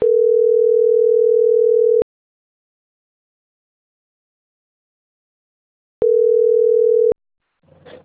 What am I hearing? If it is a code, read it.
Ringback tone (North America), 440 + 480 Hz.